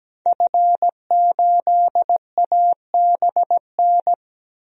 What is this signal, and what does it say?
Morse code, heard as a keyed tone, about 17 wpm, 700 Hz. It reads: F8ABN
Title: DXCC-100 Call Signs - 5 Characters - 17wpm